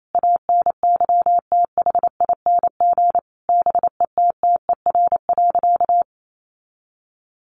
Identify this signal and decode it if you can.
Morse code, heard as a keyed tone, about 28 wpm, 700 Hz. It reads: ANYTHING BETTER.
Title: Sentences from Top 1100 Words 28wpm